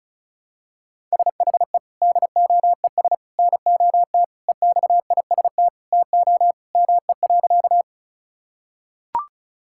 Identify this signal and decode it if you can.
Morse code, heard as a keyed tone, about 34 wpm, 700 Hz. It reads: SHE DOES NOT EXIST TO ME.
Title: Sentences from Top 1000 Words - Review Part 5 - 35wpm